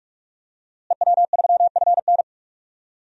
Morse code, heard as a keyed tone, about 46 wpm, 700 Hz. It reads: EW3FN
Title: DXCC-100 Call Signs - 5 Characters - Repeated 2x - 45wpm